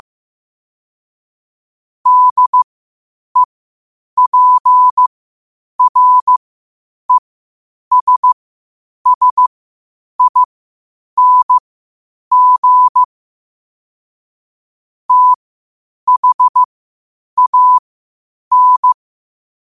Morse code, heard as a keyed tone, about 15 wpm, spaced as for 8 wpm, 1000 Hz. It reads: DEPRESSING THAN
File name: 8wpm